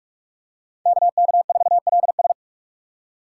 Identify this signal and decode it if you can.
Morse code, heard as a keyed tone, about 43 wpm, 700 Hz. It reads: KK4LS
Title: USA Call Signs - 3x Character Spacing - 45wpm